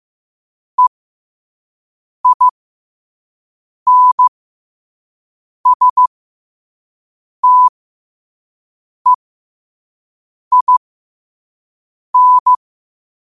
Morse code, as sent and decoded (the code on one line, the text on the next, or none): . .. -. ... - . .. -.
EINSTEIN